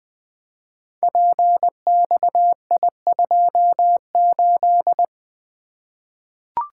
Morse code, heard as a keyed tone, about 20 wpm, 700 Hz. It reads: PXI28
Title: Code Groups - 5 Letter:Numbers - 20wpm